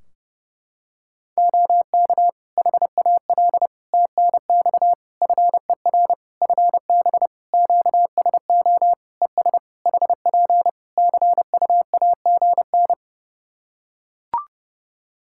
Morse code, heard as a keyed tone, about 30 wpm, 700 Hz. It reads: OK HAL TNX FER FB QSO ES HP CUAGN
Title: Practice QSO - Exchange 4 - Ending - 30wpm